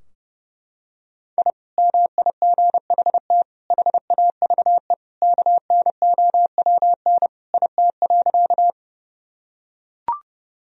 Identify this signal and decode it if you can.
Morse code, heard as a keyed tone, about 32 wpm, 700 Hz. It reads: I MIGHT HAVE KNOWN IT.
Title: Sentences from Top 1000 Words - Review Part 3 - 30wpm